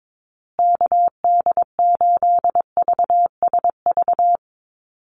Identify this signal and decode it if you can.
Morse code, heard as a keyed tone, about 22 wpm, 700 Hz. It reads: KD8VSV
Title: USA Call Signs Repeated 3x 22wpm